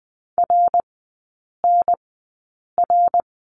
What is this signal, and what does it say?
Morse code, heard as a keyed tone, about 20 wpm, spaced as for 8 wpm, 700 Hz. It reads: RNR
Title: 053 Lesson 10e - Letter r - Three Characters 20wpm